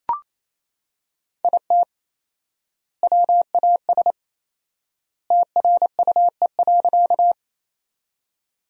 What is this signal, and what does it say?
Morse code, heard as a keyed tone, about 28 wpm, 700 Hz. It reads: IT WAS TRUE.
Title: 4x Word-Spacing - Sentences from Top 400 Words - 28wpm